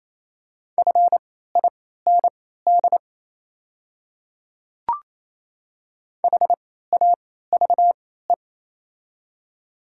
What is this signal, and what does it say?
Morse code, heard as a keyed tone, about 28 wpm, spaced as for 16 wpm, 700 Hz. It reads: FIND HAVE
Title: 3x Character Spacing - Top 100 Words - 28wpm